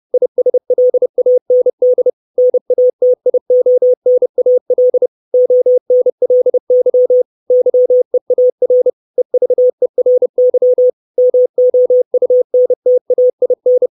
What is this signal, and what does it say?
Morse code, heard as a keyed tone, about 30 wpm, 500 Hz. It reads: ISLAND NATIONAL ONLY YEAR EVERY MOUNTAIN
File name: 632 Words 30WPM